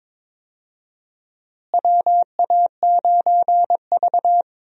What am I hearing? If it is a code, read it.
Morse code, heard as a keyed tone, about 22 wpm, 700 Hz. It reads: WA9V
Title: USA Call Signs - 4 Characters - Repeated 2x - 22wpm